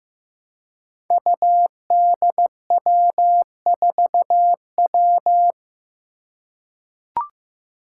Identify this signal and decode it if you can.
Morse code, heard as a keyed tone, about 15 wpm, 700 Hz. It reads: UDW4W
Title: Five Challenging Letter-Number Combinations 15wpm